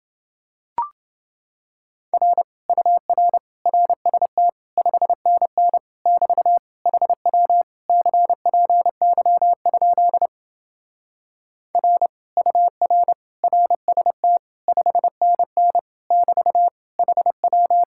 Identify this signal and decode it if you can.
Morse code, heard as a keyed tone, about 29 wpm, 700 Hz. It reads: R UR RST 5NN <BT> HW CPY? R UR RST 5NN <BT> HW